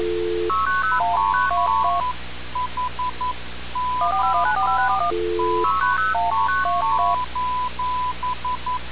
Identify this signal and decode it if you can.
Morse code, heard as a keyed tone, about 11 wpm, 1000 Hz. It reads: PHONE:
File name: radiocap / REC z